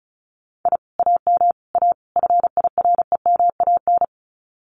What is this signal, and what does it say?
Morse code, heard as a keyed tone, about 36 wpm, 700 Hz. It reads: I AM A FIREMAN
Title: Common QSO Phrases Encore 35wpm